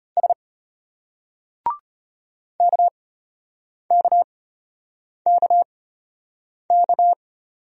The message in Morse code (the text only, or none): S KKKK